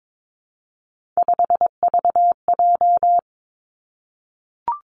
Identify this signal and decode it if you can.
Morse code, heard as a keyed tone, about 22 wpm, 700 Hz. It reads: 5VJ
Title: Three Challenging Letter-Number Combinations 22wpm